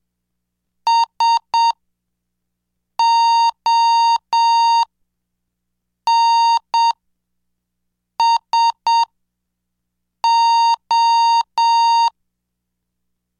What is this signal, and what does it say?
Morse code, heard as a keyed tone, about 7 wpm, spaced as for 5 wpm, 920 Hz. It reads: SONSO